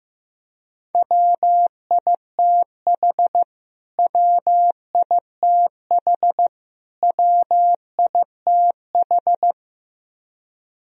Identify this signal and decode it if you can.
Morse code, heard as a keyed tone, about 16 wpm, 700 Hz. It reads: WITH WITH WITH
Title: Top 100 Words Repeated 15wpm